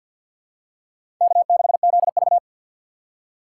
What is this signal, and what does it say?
Morse code, heard as a keyed tone, about 47 wpm, 700 Hz. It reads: K6ZV
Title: USA Call Signs - 2x Character Spacing - 50wpm